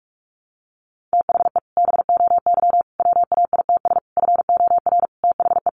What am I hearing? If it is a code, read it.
Morse code, heard as a keyed tone, about 44 wpm, 700 Hz. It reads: THE BOY WAITS FOR THE
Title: Additional Easy Phrases 45wpm